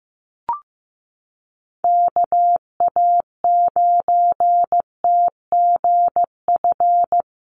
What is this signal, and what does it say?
Morse code, heard as a keyed tone, about 15 wpm, 700 Hz. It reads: KA9TGF
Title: USA Call Signs Repeated 2x 15wpm